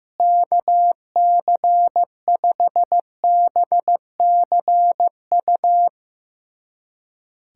Morse code, heard as a keyed tone, about 16 wpm, 700 Hz. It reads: KC5BCU
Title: USA Callsign DE Callsign - 6 Characters - 15wpm